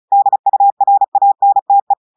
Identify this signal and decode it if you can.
Morse code, heard as a keyed tone, about 36 wpm, 800 Hz. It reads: DURANTE
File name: audios / portugais / portugais texte en clair 1 vitesse 35 mots minutes